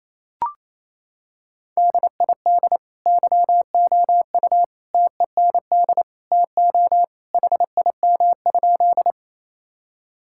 Morse code, heard as a keyed tone, about 28 wpm, 700 Hz. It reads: DID YOU TEND TO HIM?
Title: Sentences from Top 1000 Words - Review Part 5 - 28wpm